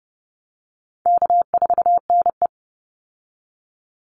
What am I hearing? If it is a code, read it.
Morse code, heard as a keyed tone, about 30 wpm, 700 Hz. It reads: K4NE
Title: Easy Call Signs Repeated 2x - 30wpm